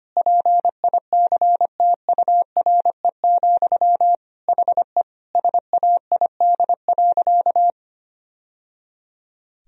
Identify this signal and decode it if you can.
Morse code, heard as a keyed tone, about 25 wpm, 700 Hz. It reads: PICTURE, HE SAID.
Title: Sentences from Top 1000 Words - Review Part 1 - 25wpm